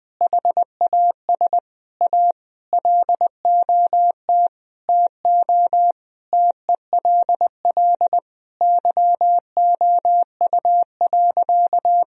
Morse code, heard as a keyed tone, about 20 wpm, 700 Hz. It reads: HAS A LOT TO TELL YOU.
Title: Sentences from Top 1000 Words - Review Part 3 - 20wpm